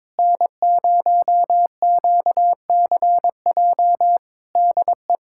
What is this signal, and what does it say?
Morse code, heard as a keyed tone, about 22 wpm, 700 Hz. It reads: N0QCJ DE
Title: K1USN SST QSO Practice 22wpm